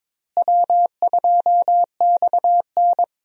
Morse code, heard as a keyed tone, about 22 wpm, 700 Hz. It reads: W2XN
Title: USA Call Signs - 4 Characters - Repeated 2x - 22wpm